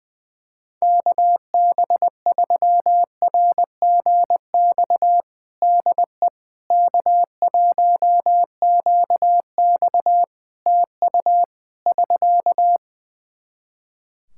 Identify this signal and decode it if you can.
Morse code, heard as a keyed tone, about 19 wpm, 700 Hz. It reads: KB3RGX DE K1QX TU <SK>
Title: Practice QSO - Exchange 4 - Ending - 20wpm